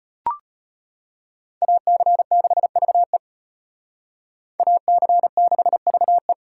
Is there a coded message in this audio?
Yes: Morse code from 1.6 to 6.3 s, heard as a keyed tone, about 32 wpm, 700 Hz. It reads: AC6VE AC6VE